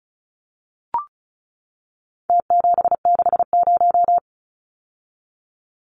Morse code, heard as a keyed tone, about 34 wpm, 700 Hz. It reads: T760
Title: Numbers - Contest Style - 4-Digit 35wpm